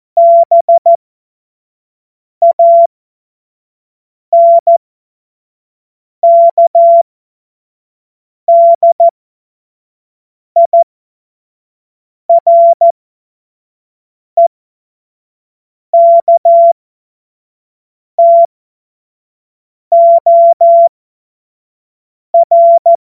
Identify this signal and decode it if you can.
Morse code, heard as a keyed tone, about 14 wpm, spaced as for 5 wpm, 680 Hz. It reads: BANKDIREKTOR
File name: Morsekurs / Witze / Witz 13 Mai 2025 25 F70